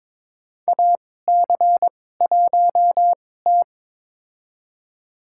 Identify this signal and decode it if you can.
Morse code, heard as a keyed tone, about 22 wpm, spaced as for 16 wpm, 700 Hz. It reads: AC1T
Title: USA Call Signs - 4 Characters - 2x Character Spacing - 22wpm